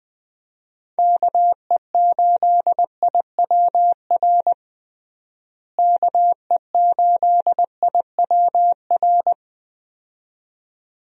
Morse code, heard as a keyed tone, about 21 wpm, 700 Hz. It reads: KE8IWR KE8IWR
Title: USA Call Signs - 6 Characters - Repeated 2x - 20wpm